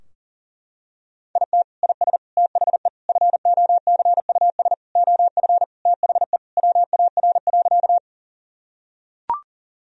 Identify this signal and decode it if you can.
Morse code, heard as a keyed tone, about 37 wpm, 700 Hz. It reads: IT IS THE FOCUS OF THE WAR.
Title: Sentences from Top 1000 Words - Review Part 4 - 40wpm